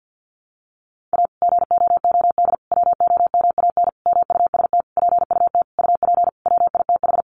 Morse code, heard as a keyed tone, about 50 wpm, 700 Hz. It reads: A GOOD WOMAN MUST PUT UP WITH
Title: Sentences from Top 1100 Words 50wpm